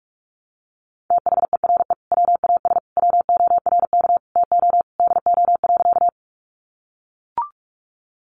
Morse code, heard as a keyed tone, about 44 wpm, 700 Hz. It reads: THERE WAS WORK TO DO.
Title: Sentences from Top 200 Words 45wpm